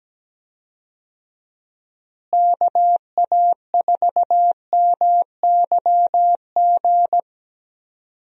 Morse code, heard as a keyed tone, about 17 wpm, 700 Hz. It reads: KA4MYG
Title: USA Call Signs - 6 Characters - Repeated 3x - 17wpm